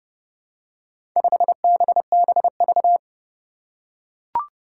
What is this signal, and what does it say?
Morse code, heard as a keyed tone, about 32 wpm, 700 Hz. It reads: 5BBV
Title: Four Challenging Letter-Number Combinations 30wpm